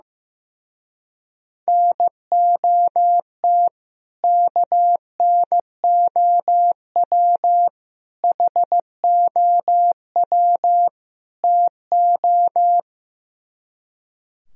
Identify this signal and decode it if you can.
Morse code, heard as a keyed tone, about 15 wpm, 700 Hz. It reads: NOT KNOW HOW TO
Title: Speed-Racing - Sets of 4 Words 15wpm